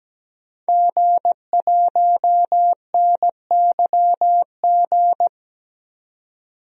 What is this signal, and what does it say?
Morse code, heard as a keyed tone, about 17 wpm, 700 Hz. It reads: G1NYG
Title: DXCC-100 Call Signs - 5 Characters - 17wpm